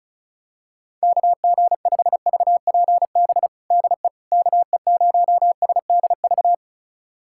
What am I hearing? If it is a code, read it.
Morse code, heard as a keyed tone, about 34 wpm, 700 Hz. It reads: KG5VPB DE KE0SDV
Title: USA Callsign DE Callsign 35wpm